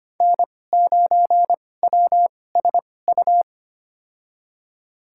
Morse code, heard as a keyed tone, about 25 wpm, spaced as for 18 wpm, 700 Hz. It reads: N9WSU
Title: USA Call Signs - 5 Characters - 2x Character Spacing - 25wpm